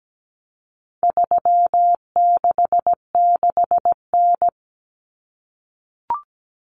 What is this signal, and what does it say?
Morse code, heard as a keyed tone, about 17 wpm, 700 Hz. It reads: 366N
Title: Common Cut Numbers - 4-Digit 17wpm